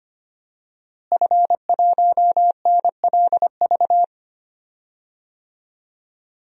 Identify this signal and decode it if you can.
Morse code, heard as a keyed tone, about 25 wpm, 700 Hz. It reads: F1NLV
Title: DXCC-100 Call Signs - 5 Characters - Repeated 3x - 25wpm